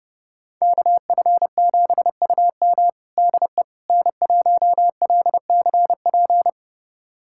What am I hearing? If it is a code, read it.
Morse code, heard as a keyed tone, about 30 wpm, 700 Hz. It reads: KF7UM DE N1LCP
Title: USA Callsign DE Callsign - 5 Characters - 30wpm